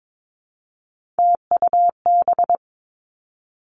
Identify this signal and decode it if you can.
Morse code, heard as a keyed tone, about 22 wpm, 700 Hz. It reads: TUB
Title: Three Challenging Letter-Number Combinations 22wpm